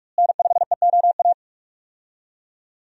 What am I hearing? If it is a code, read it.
Morse code, heard as a keyed tone, about 41 wpm, 700 Hz. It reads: N5EOA
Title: USA Call Signs - 5 Characters - Repeated 2x - 45wpm